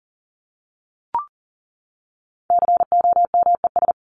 Morse code, heard as a keyed tone, about 40 wpm, 700 Hz. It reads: COMES
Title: ES:S Suffix - Top 100 Words with Base Words 40wpm